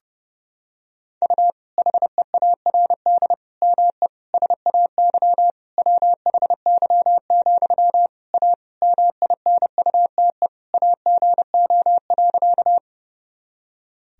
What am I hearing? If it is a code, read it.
Morse code, heard as a keyed tone, about 32 wpm, 700 Hz. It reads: U HEARD ME SAY WHY, A MINUTE AGO.
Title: Sentences from Top 500 Words and Abbreviations 30wpm